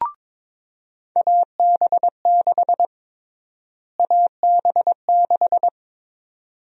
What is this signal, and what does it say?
Morse code, heard as a keyed tone, about 22 wpm, 700 Hz. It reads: AB6 AB6
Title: USA Call Sign Prefixes - Repeated 2x - 22wpm